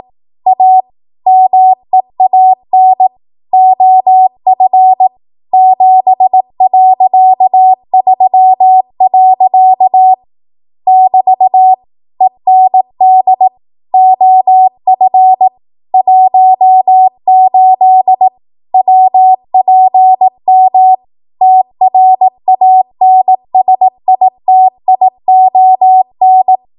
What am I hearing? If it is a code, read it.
Morse code, heard as a keyed tone, about 18 wpm, 750 Hz. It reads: A MEAN OF 7.3. <BT> END OF 18 WPM TRANSITION
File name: Archive / 18 WPM / 18WPM